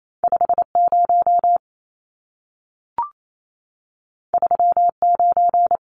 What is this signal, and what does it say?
Morse code, heard as a keyed tone, about 28 wpm, 700 Hz. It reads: 50 39
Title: Numbers - 2-Digit 28wpm